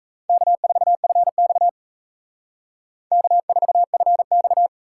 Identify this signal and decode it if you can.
Morse code, heard as a keyed tone, about 35 wpm, 700 Hz. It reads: K4FX K4FX